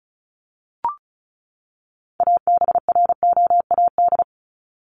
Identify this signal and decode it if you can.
Morse code, heard as a keyed tone, about 34 wpm, 700 Hz. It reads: ABROAD
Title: Top 1751-2000 Words 35wpm